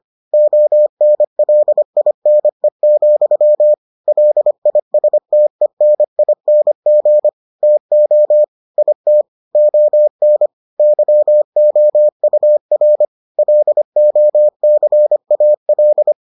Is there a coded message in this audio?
Yes: Morse code from 0.3 to 16.1 s, heard as a keyed tone, about 25 wpm, 600 Hz. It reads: ONLINE, LISTENING TO IT ON YOUR LOCAL